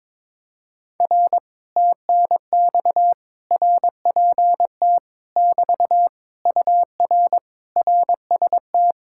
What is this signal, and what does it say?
Morse code, heard as a keyed tone, about 22 wpm, 700 Hz. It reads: R TNX RPT <BT> UR RST
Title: Speed-Racing Common QSO Phrases - 22wpm